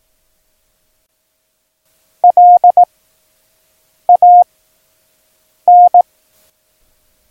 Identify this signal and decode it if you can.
Morse code, heard as a keyed tone, about 18 wpm, 710 Hz. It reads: L A N
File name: acs / Instrument Rating / Instrument ACS